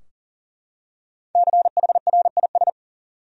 Morse code, heard as a keyed tone, about 39 wpm, 700 Hz. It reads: CHRIS
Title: USA Common Names - 40wpm